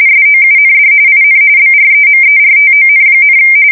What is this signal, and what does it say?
RTTY, about 45.45 baud. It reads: SOUND IN BREMERTON,
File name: fsk170 / USS-New-Jersey